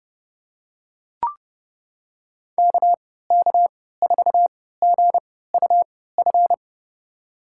Morse code, heard as a keyed tone, about 32 wpm, spaced as for 17 wpm, 700 Hz. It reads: KK4GUF
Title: USA Call Signs - 6 Characters - 3x Character Spacing - 30wpm